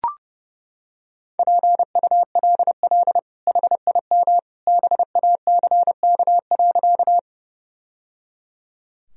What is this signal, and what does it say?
Morse code, heard as a keyed tone, about 30 wpm, 700 Hz. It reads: PULL HIM BACK.
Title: Sentences from Top 1000 Words - Review Part 2 - 30wpm